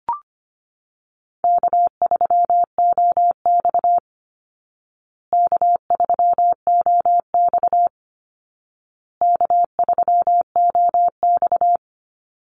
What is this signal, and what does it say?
Morse code, heard as a keyed tone, about 25 wpm, 700 Hz. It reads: K3OX K3OX K3OX
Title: Easy Call Signs Repeated 3x - 25wpm